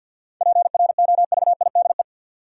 Morse code, heard as a keyed tone, about 53 wpm, 700 Hz. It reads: PROVIDE
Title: Sentences from Top 1100 Words - New Word Drill Alt - 50wpm